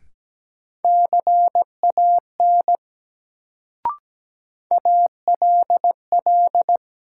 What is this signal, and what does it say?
Morse code, heard as a keyed tone, about 17 wpm, 700 Hz. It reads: CAN ALL
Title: Sets of 2 Words - New Word Drill - 17wpm